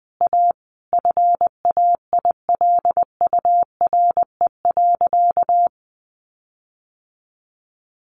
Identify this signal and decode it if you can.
Morse code, heard as a keyed tone, about 20 wpm, 700 Hz. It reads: A FAILURE.
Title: Sentences from Top 1300 Words 20wpm